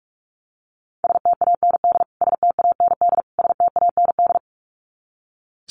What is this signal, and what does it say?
Morse code, heard as a keyed tone, about 44 wpm, 700 Hz. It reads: STAND STAND STAND
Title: Top 300 Words Repeated 45wpm